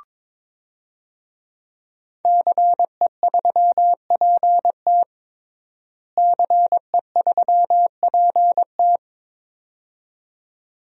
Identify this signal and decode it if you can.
Morse code, heard as a keyed tone, about 22 wpm, 700 Hz. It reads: CE3PT CE3PT
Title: DXCC-100 Call Signs - 5 Characters - Repeated 2x - 22wpm